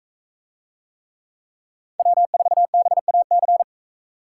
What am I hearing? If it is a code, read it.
Morse code, heard as a keyed tone, about 44 wpm, 700 Hz. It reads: W4BAC